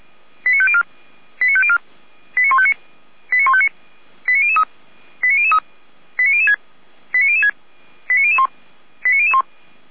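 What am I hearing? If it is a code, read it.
Five-tone selective call (ZVEI), heard as quick runs of five tones, tones of about 70 ms. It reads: 89504 89504 89159 89159 89003 89003 89006 89006 89001 89001
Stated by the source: ZVEI five-tone selective call received via SDR, from the Feuerwehr Orlamünde audio archive.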